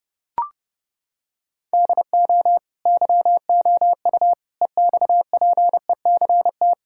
Morse code, heard as a keyed tone, about 32 wpm, 700 Hz. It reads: DO YOU EXPECT
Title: Sentences from Top 1600 Words 30wpm